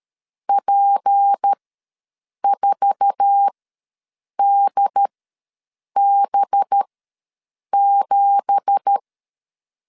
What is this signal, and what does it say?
Morse code, heard as a keyed tone, about 13 wpm, spaced as for 7 wpm, 800 Hz. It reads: P4DB7
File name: LEZIONI / 10 SIGNS / 10A SLASH ws 12wpm 3443